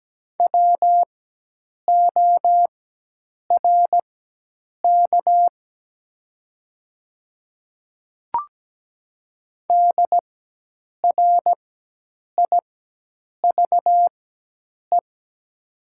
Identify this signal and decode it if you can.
Morse code, heard as a keyed tone, about 17 wpm, spaced as for 8 wpm, 700 Hz. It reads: WORK DRIVE
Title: 4x Character Spacing - Top 500 Words - 17wpm